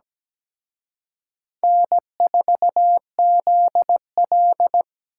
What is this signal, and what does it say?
Morse code, heard as a keyed tone, about 20 wpm, 700 Hz. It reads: N4ZL
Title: Speed Racing - USA 1x2 Call Signs 17wpm